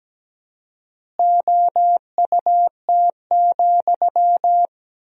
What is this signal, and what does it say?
Morse code, heard as a keyed tone, about 17 wpm, 700 Hz. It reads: OUT,
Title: 4x Word-Spacing - Sentences from Top 500 Words - 17wpm